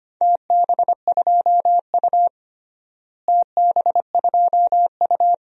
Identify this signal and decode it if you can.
Morse code, heard as a keyed tone, about 25 wpm, 700 Hz. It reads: TB2U TB2U
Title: DXCC-100 Call Signs - 4 Characters - Repeated 2x - 25wpm